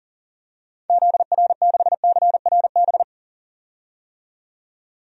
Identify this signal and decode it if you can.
Morse code, heard as a keyed tone, about 40 wpm, 700 Hz. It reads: ZR6CRB
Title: DXCC-100 Call Signs - 6 Characters - 40wpm